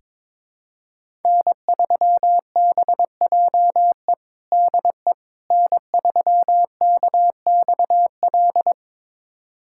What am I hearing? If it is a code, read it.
Morse code, heard as a keyed tone, about 21 wpm, 700 Hz. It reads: N3BJE DE N3KXL
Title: USA Callsign DE Callsign - 5 Characters - 22wpm